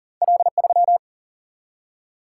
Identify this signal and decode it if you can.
Morse code, heard as a keyed tone, about 42 wpm, 700 Hz. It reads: L3
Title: Single-Letter Single-Number 40wpm